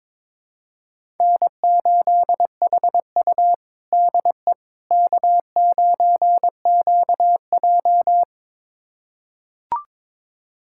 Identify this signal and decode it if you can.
Morse code, heard as a keyed tone, about 22 wpm, 700 Hz. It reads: N8HU DE K9QJ
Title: USA Callsign DE Callsign - 4 Characters - 22wpm